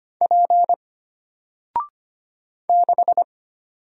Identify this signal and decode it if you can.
Morse code, heard as a keyed tone, about 25 wpm, 700 Hz. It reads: P 6
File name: Single-Letter-Number-25wpm